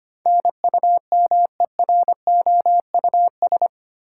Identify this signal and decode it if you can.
Morse code, heard as a keyed tone, about 25 wpm, 700 Hz. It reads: NUMEROUS
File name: Eight Letter Words - 25wpm